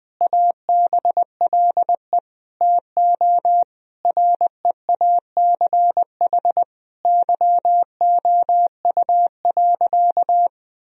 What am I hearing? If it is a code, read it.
Morse code, heard as a keyed tone, about 21 wpm, 700 Hz. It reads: ABLE TO REACH YOU.
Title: Sentences from Top 1000 Words - Review Part 2 - 20wpm